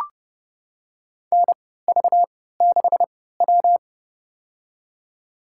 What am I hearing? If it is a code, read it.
Morse code, heard as a keyed tone, about 30 wpm, spaced as for 17 wpm, 700 Hz. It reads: NV6W
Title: USA Call Signs - 4 Characters - 3x Character Spacing - 30wpm